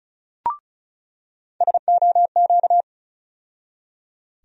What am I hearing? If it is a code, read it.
Morse code, heard as a keyed tone, about 34 wpm, 700 Hz. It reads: SOQ
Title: USA Call Sign Postfixes - 3 Letters 35wpm